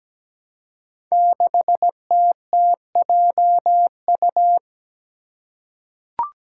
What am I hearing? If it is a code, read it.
Morse code, heard as a keyed tone, about 17 wpm, 700 Hz. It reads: 6TTJU
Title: Five Challenging Letter-Number Combinations 17wpm